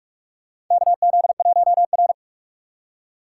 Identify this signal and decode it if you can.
Morse code, heard as a keyed tone, about 46 wpm, 700 Hz. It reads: KZ1R
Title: USA Call Signs - 4 Characters - 3x Character Spacing - 45wpm